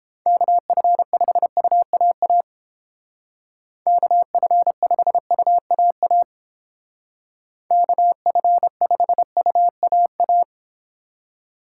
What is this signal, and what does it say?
Morse code, heard as a keyed tone, about 31 wpm, 700 Hz. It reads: KF5UAA KF5UAA KF5UAA